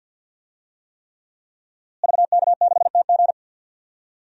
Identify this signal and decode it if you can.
Morse code, heard as a keyed tone, about 51 wpm, 700 Hz. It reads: VK6TG
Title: DXCC-100 Call Signs - 5 Characters - Repeated 3x - 50wpm